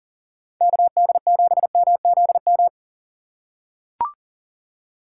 Morse code, heard as a keyed tone, about 40 wpm, 700 Hz. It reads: KD7MZM
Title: USA Call Signs - 6 Characters - 2x Character Spacing - 40wpm